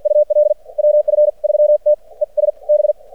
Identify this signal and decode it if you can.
Morse code, heard as a keyed tone, about 56 wpm, spaced as for 45 wpm, 600 Hz. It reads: UL WF3T ES B
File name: archived / CRSnet